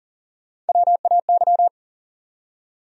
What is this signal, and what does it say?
Morse code, heard as a keyed tone, about 57 wpm, spaced as for 48 wpm, 700 Hz. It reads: WAY
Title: Speed-Racing Top 500 Words 40wpm